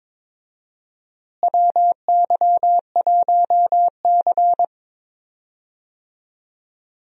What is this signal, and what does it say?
Morse code, heard as a keyed tone, about 21 wpm, 700 Hz. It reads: WY1C
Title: USA Call Signs - 4 Characters - Repeated 3x - 22wpm